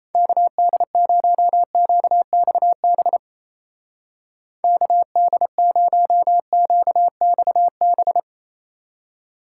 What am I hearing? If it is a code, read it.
Morse code, heard as a keyed tone, about 36 wpm, 700 Hz. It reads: KD0QXB KD0QXB